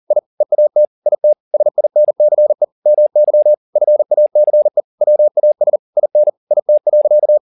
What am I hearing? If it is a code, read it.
Morse code, heard as a keyed tone, about 40 wpm, 600 Hz. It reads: I EAT IT SINCE MY FACE WAS IN IT.
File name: the old man and the sea 2 40wpm